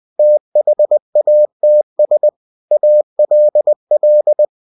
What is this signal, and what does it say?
Morse code, heard as a keyed tone, about 20 wpm, 600 Hz. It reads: THATS ALL
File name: cw-books / winnie-the-pooh / winnie-20wpm-page-0045